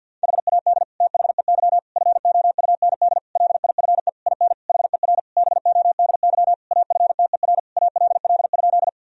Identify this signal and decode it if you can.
Morse code, heard as a keyed tone, about 52 wpm, 700 Hz. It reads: HAD THEY FOUND LIFE IN HER BODY AFTER ALL?
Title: Sentences from Top 1000 Words - Review Part 2 - 50wpm